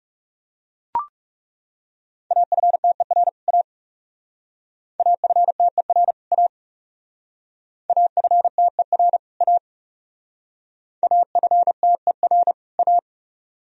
Morse code, heard as a keyed tone, about 37 wpm, 700 Hz. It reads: AFTER A AFTER A AFTER A AFTER A
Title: Speed-Racing - Sets of 2 Words 30wpm